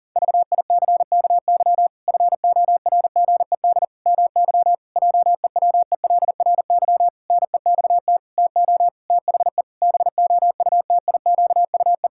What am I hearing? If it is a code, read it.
Morse code, heard as a keyed tone, about 40 wpm, 700 Hz. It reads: VICKY FORGED MY JEWELRY NEXT TO THE BOUTIQUE
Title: Pangram Challenge v2 - 40wpm